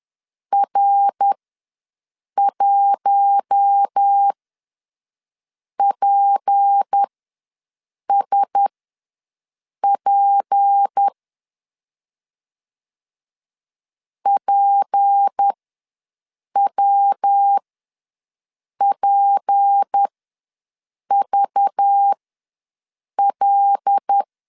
Morse code, heard as a keyed tone, about 10 wpm, spaced as for 6 wpm, 800 Hz. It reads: R1PSP PWPVL
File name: LEZIONI / 6 PKX0 / 6 P ws 10wpm 3751